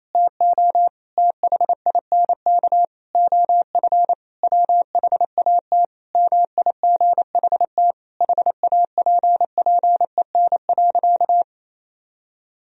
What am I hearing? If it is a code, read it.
Morse code, heard as a keyed tone, about 28 wpm, 700 Hz. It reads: TO THINK OF WHAT MIGHT HAPPEN.
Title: Sentences from Top 400 Words and Abbreviations 28wpm